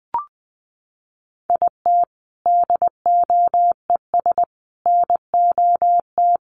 Morse code, heard as a keyed tone, about 20 wpm, 700 Hz. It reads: IT DOES NOT
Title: Sets of 5 Words Encore2 20wpm